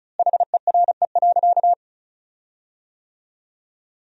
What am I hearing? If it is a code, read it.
Morse code, heard as a keyed tone, about 34 wpm, 700 Hz. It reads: HERE.
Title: 4x Word-Spacing - Sentences from Top 500 Words - 35wpm